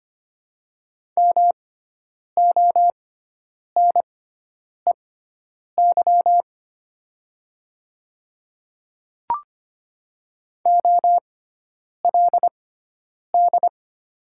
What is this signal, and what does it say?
Morse code, heard as a keyed tone, about 25 wpm, spaced as for 9 wpm, 700 Hz. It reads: MONEY OLD
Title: 6x Character Spacing - Top 500 Words - 25wpm